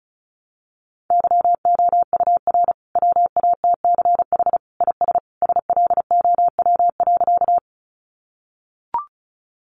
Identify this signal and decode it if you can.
Morse code, heard as a keyed tone, about 34 wpm, 700 Hz. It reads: YOUR WATCH IS SLOW.
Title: Sentences from Top 400 Words 35wpm